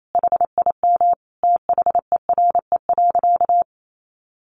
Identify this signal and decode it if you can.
Morse code, heard as a keyed tone, about 28 wpm, 700 Hz. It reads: HIM THERE.
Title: Sentences from Top 1100 Words 28wpm